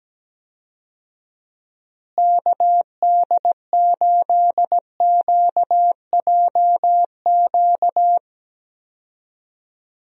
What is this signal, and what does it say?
Morse code, heard as a keyed tone, about 17 wpm, 700 Hz. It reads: KD8QJQ
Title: USA Call Signs - 6 Characters - Repeated 2x - 17wpm